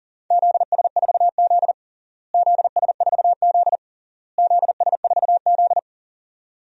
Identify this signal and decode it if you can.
Morse code, heard as a keyed tone, about 40 wpm, 700 Hz. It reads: ZS4Z ZS4Z ZS4Z
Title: DXCC-100 Call Signs - 4 Characters - Repeated 3x - 40wpm